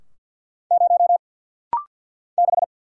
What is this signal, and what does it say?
Morse code, heard as a keyed tone, about 49 wpm, 700 Hz. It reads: 0 6
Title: Single Letter-Number Alt - Rapid-Fire 50wpm